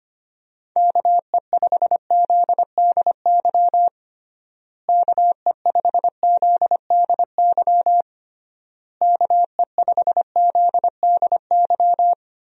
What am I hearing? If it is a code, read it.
Morse code, heard as a keyed tone, about 25 wpm, 700 Hz. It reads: KE5ZDY KE5ZDY KE5ZDY
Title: USA Call Signs - 6 Characters - Repeated 3x - 25wpm